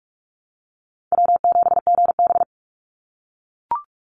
Morse code, heard as a keyed tone, about 44 wpm, 700 Hz. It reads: W7GB
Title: Call Signs - 45wpm